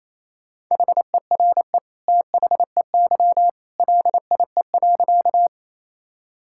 Morse code, heard as a keyed tone, about 28 wpm, 700 Hz. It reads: HERE THEY LIE.
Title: Sentences from Top 1600 Words 28wpm